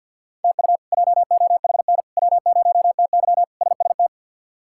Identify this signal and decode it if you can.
Morse code, heard as a keyed tone, about 51 wpm, 700 Hz. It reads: TU JOHN W0TY SST
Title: K1USN SST QSO Practice - 50wpm